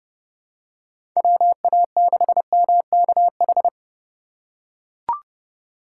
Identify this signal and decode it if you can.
Morse code, heard as a keyed tone, about 30 wpm, 700 Hz. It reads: WA6MKH
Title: USA Call Signs - 6 Characters - 2x Character Spacing - 30wpm